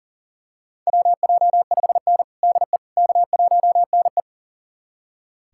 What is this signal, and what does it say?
Morse code, heard as a keyed tone, about 40 wpm, 700 Hz. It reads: WJ5N DE K1NE
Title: USA Callsign DE Callsign - 4 Characters - 40wpm